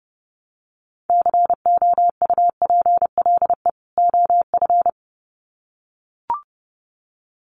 Morse code, heard as a keyed tone, about 30 wpm, 700 Hz. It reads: COUPLE OF
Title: Sets of 2 Words 30wpm